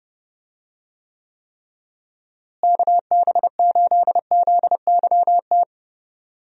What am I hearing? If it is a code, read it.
Morse code, heard as a keyed tone, about 32 wpm, 700 Hz. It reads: KB8ZYT
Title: USA Call Signs - 6 Characters - Repeated 3x - 30wpm